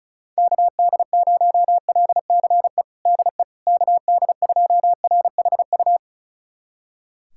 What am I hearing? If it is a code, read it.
Morse code, heard as a keyed tone, about 36 wpm, 700 Hz. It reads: KD0LCE DE KD2RHU
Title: USA Callsign DE Callsign - 6 Characters - 35wpm